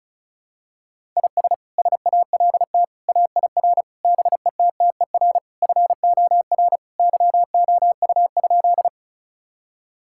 Morse code, heard as a keyed tone, about 34 wpm, 700 Hz. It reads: IS SALT AIR BETTER FOR YOU?
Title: Sentences from Top 1000 Words - Review Part 5 - 35wpm